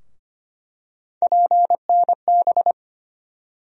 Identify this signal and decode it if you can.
Morse code, heard as a keyed tone, about 25 wpm, 700 Hz. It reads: PNB
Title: USA Call Sign Postfixes - 3 Letters 25wpm